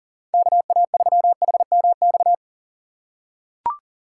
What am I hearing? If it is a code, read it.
Morse code, heard as a keyed tone, about 40 wpm, 700 Hz. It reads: KA3HMX
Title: USA Call Signs - 6 Characters - 6x Character Spacing - 40wpm